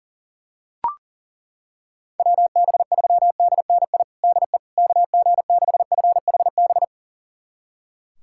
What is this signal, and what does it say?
Morse code, heard as a keyed tone, about 40 wpm, 700 Hz. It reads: WB3DNI DE KG6FHB
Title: USA Callsign DE Callsign - 6 Characters - 40wpm